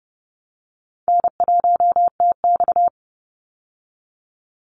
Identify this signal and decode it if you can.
Morse code, heard as a keyed tone, about 30 wpm, 700 Hz. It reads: N1TX
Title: Call Signs Repeated 3x - 30wpm